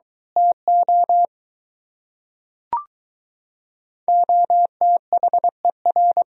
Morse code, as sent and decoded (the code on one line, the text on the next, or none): - --- / --- - .... .-.
TO OTHER